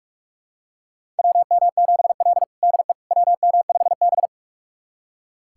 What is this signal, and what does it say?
Morse code, heard as a keyed tone, about 46 wpm, 700 Hz. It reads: WM7L DE WM5B
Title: USA Callsign DE Callsign - 4 Characters - 45wpm